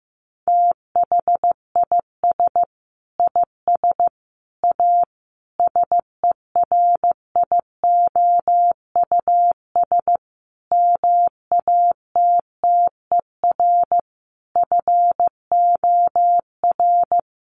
Morse code, heard as a keyed tone, about 15 wpm, 700 Hz. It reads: THIS IS A SERIOUS MATTER FOR
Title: Sentences from Top 1100 Words 15wpm